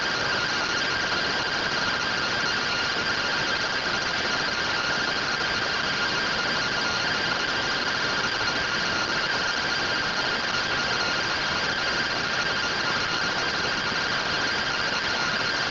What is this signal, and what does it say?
Two-tone teleprinter signal, about 170 Hz shift.